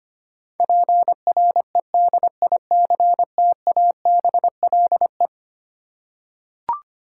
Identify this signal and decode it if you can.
Morse code, heard as a keyed tone, about 25 wpm, 700 Hz. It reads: PREDICTABLE
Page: morse-code-ninja-practice-25wpm